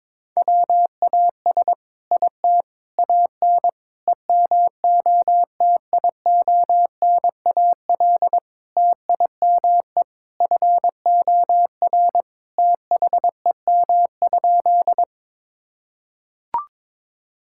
Morse code, heard as a keyed tone, about 22 wpm, 700 Hz. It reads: WAS IT AN EMOTIONAL TIME FOR THEM?
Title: Sentences from Top 1000 Words - Review Part 5 - 22wpm